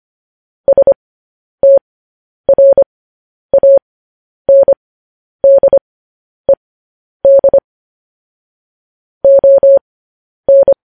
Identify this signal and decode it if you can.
Morse code, heard as a keyed tone, about 25 wpm, spaced as for 10 wpm, 550 Hz. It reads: STRANDED ON
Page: News Headlines in Morse Code at 10 WPM